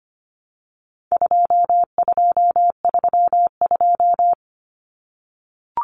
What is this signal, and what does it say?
Morse code, heard as a keyed tone, about 25 wpm, 700 Hz. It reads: 2232
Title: Common Cut Numbers - 4-Digit 25wpm